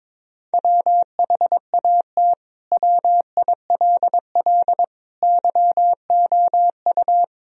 Morse code, heard as a keyed tone, about 22 wpm, 700 Hz. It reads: WHAT WILL YOU
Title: Sentences from Top 1000 Words - Review Part 4 - 22wpm